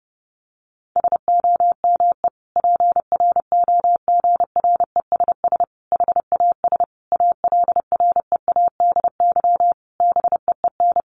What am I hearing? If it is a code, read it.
Morse code, heard as a keyed tone, about 30 wpm, 700 Hz. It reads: SOME PROGRESS HAS ALREADY BEEN
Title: Sentences from Top 1400 Words 30wpm